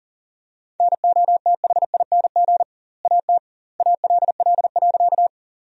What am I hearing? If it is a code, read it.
Morse code, heard as a keyed tone, about 40 wpm, 700 Hz. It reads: NOTHING AT ALL.
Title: 2x Word-Spacing - Sentences from Top 500 Words - 40wpm